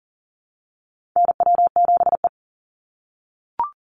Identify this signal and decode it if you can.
Morse code, heard as a keyed tone, about 40 wpm, 700 Hz. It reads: NW7E
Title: NA SOTA Chaser Callsigns 40wpm